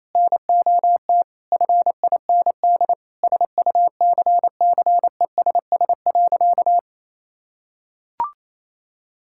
Morse code, heard as a keyed tone, about 28 wpm, 700 Hz. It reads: NOT FIND SUCCESS.
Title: Sentences from Top 1000 Words - Review Part 5 - 28wpm